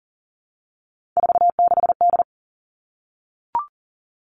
Morse code, heard as a keyed tone, about 43 wpm, 700 Hz. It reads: 46D
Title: Three Challenging Letter-Number Combinations 40wpm